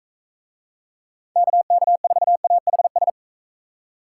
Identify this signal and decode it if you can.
Morse code, heard as a keyed tone, about 43 wpm, 700 Hz. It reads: KK4AHS